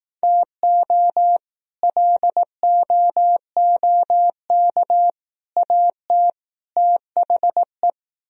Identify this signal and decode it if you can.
Morse code, heard as a keyed tone, about 18 wpm, 700 Hz. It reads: TO LOOK AT THE
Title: Speed-Racing - Sets of 4 Words 15wpm